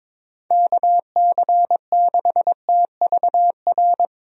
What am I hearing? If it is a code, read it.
Morse code, heard as a keyed tone, about 22 wpm, 700 Hz. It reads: KC6TVR
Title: USA Call Signs - 6 Characters - Repeated 3x - 22wpm